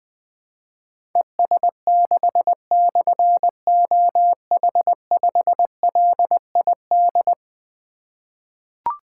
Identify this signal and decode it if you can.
Morse code, heard as a keyed tone, about 20 wpm, 700 Hz. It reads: ES6/OH5LID
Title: Difficult Call Signs 20wpm